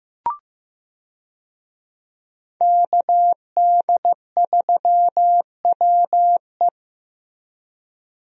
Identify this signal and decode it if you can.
Morse code, heard as a keyed tone, about 15 wpm, 700 Hz. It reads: KD3WE
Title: USA Call Signs - 5 Characters - Repeated 3x - 15wpm